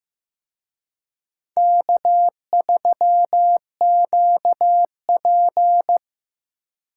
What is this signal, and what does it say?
Morse code, heard as a keyed tone, about 18 wpm, 700 Hz. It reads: K3QP
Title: Speed Racing - USA 1x2 Call Signs 15wpm